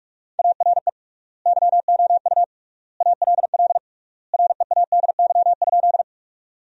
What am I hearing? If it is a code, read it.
Morse code, heard as a keyed tone, about 46 wpm, 700 Hz. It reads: ARE YOU ALL READY?
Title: 3x Word-Spacing - Sentences from Top 400 Words - 45wpm